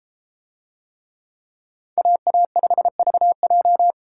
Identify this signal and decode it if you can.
Morse code, heard as a keyed tone, about 33 wpm, 700 Hz. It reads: AA5VJ